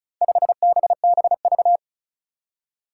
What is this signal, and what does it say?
Morse code, heard as a keyed tone, about 35 wpm, 700 Hz. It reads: 5BBV
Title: Four Challenging Letter-Number Combinations 35wpm